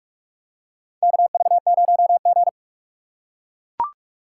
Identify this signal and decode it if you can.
Morse code, heard as a keyed tone, about 43 wpm, 700 Hz. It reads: KV0G
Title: USA Call Signs - 4 Characters - 3x Character Spacing - 45wpm